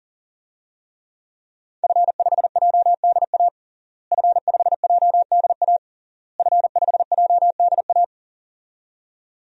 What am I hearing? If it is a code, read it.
Morse code, heard as a keyed tone, about 40 wpm, 700 Hz. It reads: F5JDA F5JDA F5JDA
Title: DXCC-100 Call Signs - 5 Characters - Repeated 3x - 40wpm